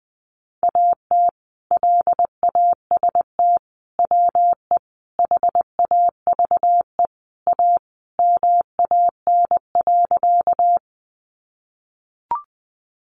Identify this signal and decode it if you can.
Morse code, heard as a keyed tone, about 20 wpm, 700 Hz. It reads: AT LAST WE HAVE A MAN.
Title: Sentences from Top 300 Words 20wpm